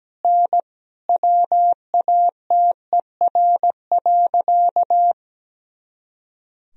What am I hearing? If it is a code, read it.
Morse code, heard as a keyed tone, about 17 wpm, 700 Hz. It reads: N WATER.
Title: Sentences from Top 100 Words and Abbreviations 17wpm